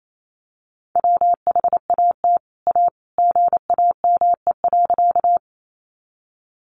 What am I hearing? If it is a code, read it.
Morse code, heard as a keyed tone, about 28 wpm, 700 Hz. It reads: WHAT A GAME.
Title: Sentences from Top 500 Words 28wpm